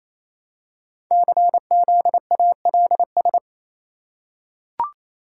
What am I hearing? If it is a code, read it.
Morse code, heard as a keyed tone, about 27 wpm, 700 Hz. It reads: CZALS
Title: Code Groups - 5 Letters - 28wpm